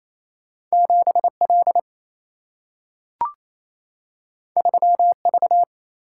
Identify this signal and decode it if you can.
Morse code, heard as a keyed tone, about 28 wpm, 700 Hz. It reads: 7L 3V
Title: Single-Letter Single-Number 28wpm